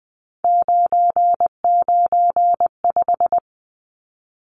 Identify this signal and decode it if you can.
Morse code, heard as a keyed tone, about 20 wpm, 700 Hz. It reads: 995
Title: Numbers - 3-Digit 20wpm